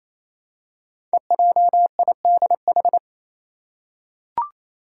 Morse code, heard as a keyed tone, about 27 wpm, 700 Hz. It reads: EJIDH
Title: Code Groups - 5 Letters - 28wpm